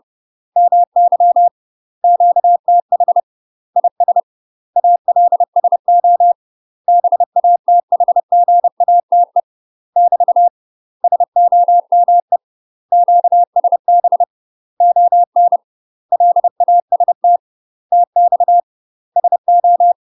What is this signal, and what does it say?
Morse code, heard as a keyed tone, about 30 wpm, 700 Hz. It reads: MY QTH IS ALSO BATHGATE <BT> SOME QSB ON LAST TX SO
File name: UK MASTER QSOs TRAINING / 30WPM / UK MASTER QSOs 0039